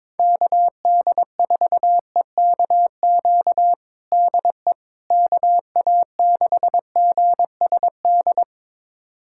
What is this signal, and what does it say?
Morse code, heard as a keyed tone, about 21 wpm, 700 Hz. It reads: KD4EKQ DE KA6GSD
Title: USA Callsign DE Callsign - 6 Characters - 22wpm